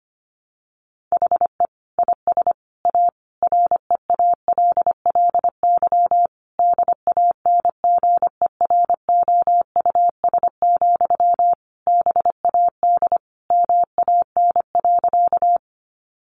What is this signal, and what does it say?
Morse code, heard as a keyed tone, about 25 wpm, 700 Hz. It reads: HE IS A REALLY DANGEROUS, BAD MAN.
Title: Sentences from Top 1200 Words 25wpm